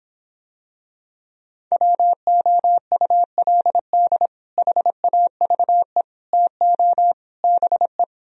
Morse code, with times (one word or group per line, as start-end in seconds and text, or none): 1.7-4.3 WOULD
4.6-6.0 HAVE
6.3-7.1 TO
7.4-8.0 BE